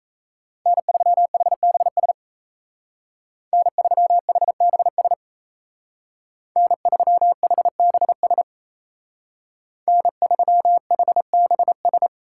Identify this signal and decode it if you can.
Morse code, heard as a keyed tone, about 36 wpm, 700 Hz. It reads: N3HBS N3HBS N3HBS N3HBS